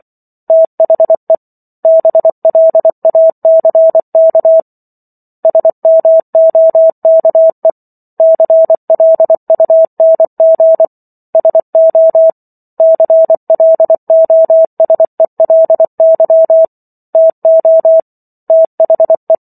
Morse code, heard as a keyed tone, about 24 wpm, 650 Hz. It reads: THE BLACK SMOKE CLUNG SO CLOSELY TO THE